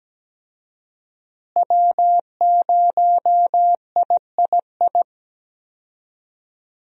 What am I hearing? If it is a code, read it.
Morse code, heard as a keyed tone, about 17 wpm, 700 Hz. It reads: W0III
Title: USA Call Signs - 5 Characters - Repeated 3x - 17wpm